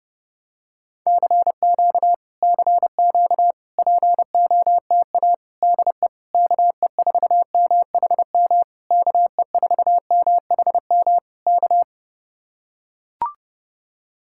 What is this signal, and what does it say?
Morse code, heard as a keyed tone, about 30 wpm, 700 Hz. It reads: CQ CQ POTA DE KE4MHM KE4MHM K